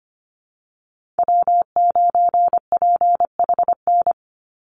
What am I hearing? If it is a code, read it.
Morse code, heard as a keyed tone, about 25 wpm, 700 Hz. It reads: W9PHN
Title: USA Call Signs 25wpm